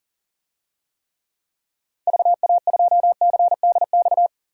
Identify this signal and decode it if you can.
Morse code, heard as a keyed tone, about 40 wpm, 700 Hz. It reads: VA2CDX
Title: DXCC-100 Call Signs - 6 Characters - 40wpm